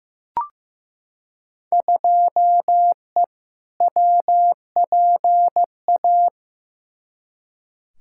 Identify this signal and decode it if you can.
Morse code, heard as a keyed tone, about 15 wpm, 700 Hz. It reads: 2E WPA
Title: Field Day Exchanges Encore - 15wpm